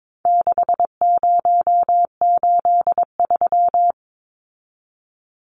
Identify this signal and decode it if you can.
Morse code, heard as a keyed tone, about 22 wpm, 700 Hz. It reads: 6083
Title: Numbers - 4-Digit 22wpm